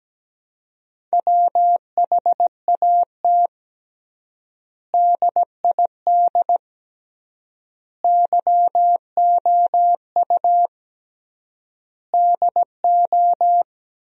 Morse code, heard as a keyed tone, about 17 wpm, 700 Hz. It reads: WHAT DID YOU DO
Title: 3x Word-Spacing - Sentences from Top 100 Words - 17wpm